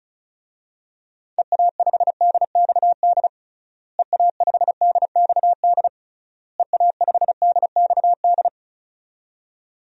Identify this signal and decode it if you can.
Morse code, heard as a keyed tone, about 34 wpm, 700 Hz. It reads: EA5DXD EA5DXD EA5DXD
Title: DXCC-100 Call Signs - 6 Characters - Repeated 3x - 35wpm